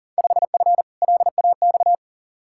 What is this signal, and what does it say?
Morse code, heard as a keyed tone, about 40 wpm, 700 Hz. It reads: 5F LAX
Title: Field Day Exchanges Encore - 40wpm